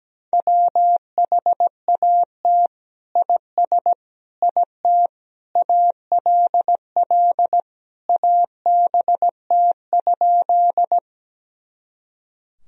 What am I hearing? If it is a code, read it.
Morse code, heard as a keyed tone, about 17 wpm, 700 Hz. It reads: WHAT IS IT ALL ABT?
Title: Sentences from Top 100 Words and Abbreviations 17wpm